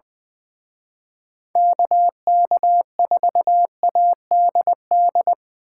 Morse code, heard as a keyed tone, about 25 wpm, 700 Hz. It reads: KK4ADD
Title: Speed Racing - USA 2x3 Call Signs 20wpm